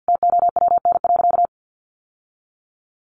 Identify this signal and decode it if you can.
Morse code, heard as a keyed tone, about 50 wpm, 700 Hz. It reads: TOWN.
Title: Sentences from Top 1500 Words 50wpm